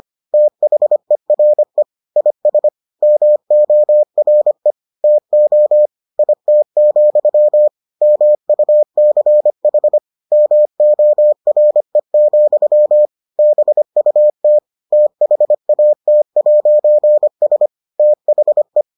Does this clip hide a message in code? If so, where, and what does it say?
Yes: Morse code from 0.3 to 18.8 s, heard as a keyed tone, about 25 wpm, 600 Hz. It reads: THERE IS MORE TO IT, MUCH MORE, BUT THAT'S THE